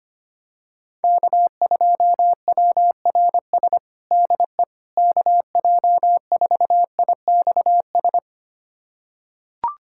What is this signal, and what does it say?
Morse code, heard as a keyed tone, about 25 wpm, 700 Hz. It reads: K2WRS DE KJ4IXS
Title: USA Callsign DE Callsign 25wpm